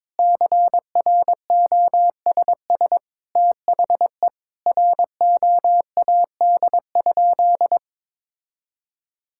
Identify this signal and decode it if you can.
Morse code, heard as a keyed tone, about 22 wpm, 700 Hz. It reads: CROSS THE ROAD?
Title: Sentences from Top 1000 Words - Review Part 5 - 22wpm